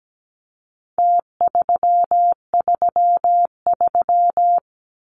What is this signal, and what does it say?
Morse code, heard as a keyed tone, about 17 wpm, 700 Hz. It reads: T333
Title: Numbers - Contest Style - 4-Digit 17wpm